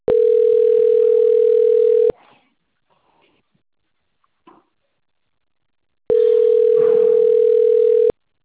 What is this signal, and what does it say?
Ringback tone (North America), 440 + 480 Hz.